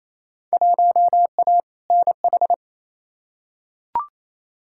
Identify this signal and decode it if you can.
Morse code, heard as a keyed tone, about 28 wpm, 700 Hz. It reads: 1A NH
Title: Field Day Exchanges Encore - 28wpm